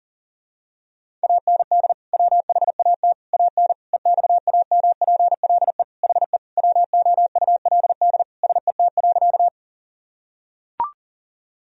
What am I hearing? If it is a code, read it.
Morse code, heard as a keyed tone, about 40 wpm, 700 Hz. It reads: AND WHAT AN EXAMPLE HE WOULD SET.
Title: Sentences from Top 1000 Words - Review Part 1 - 40wpm